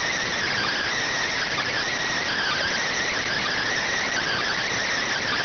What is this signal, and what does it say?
Emergency Alert System (SAME) tones, heard as harsh buzzing bursts. It is heard from 0.5 to 4.4 s.